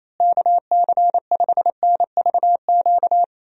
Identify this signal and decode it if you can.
Morse code, heard as a keyed tone, about 28 wpm, 700 Hz. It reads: KC5NVQ
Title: USA Call Signs - 6 Characters - Repeated 2x - 28wpm